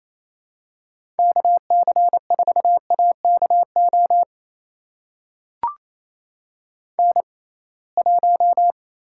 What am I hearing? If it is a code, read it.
Morse code, heard as a keyed tone, about 28 wpm, spaced as for 21 wpm, 700 Hz. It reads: KC4AKO N1
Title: USA Call Signs - 6x Character Spacing - 28wpm